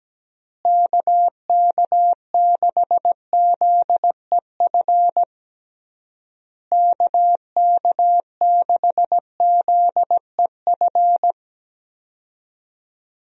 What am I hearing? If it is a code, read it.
Morse code, heard as a keyed tone, about 17 wpm, 700 Hz. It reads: KK6ZEF KK6ZEF
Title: USA Call Signs - 6 Characters - Repeated 3x - 17wpm